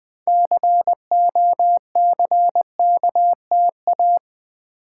Morse code, heard as a keyed tone, about 20 wpm, 700 Hz. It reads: COCKTA
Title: Build A Word - 2 Letters At A Time - 8-Letter Words - 20wpm